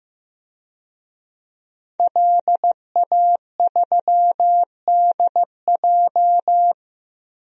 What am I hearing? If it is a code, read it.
Morse code, heard as a keyed tone, about 15 wpm, 700 Hz. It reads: LA3DJ
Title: DXCC-100 Call Signs - 5 Characters - Repeated 2x - 15wpm